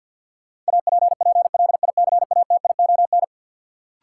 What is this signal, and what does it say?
Morse code, heard as a keyed tone, about 53 wpm, 700 Hz. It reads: APPLICATION
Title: Sentences from Top 1200 Words - New Word Drill - 50wpm